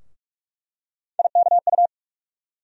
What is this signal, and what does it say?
Morse code, heard as a keyed tone, about 41 wpm, 700 Hz. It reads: IKU
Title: USA Call Sign Postfixes - 3 Letters 45wpm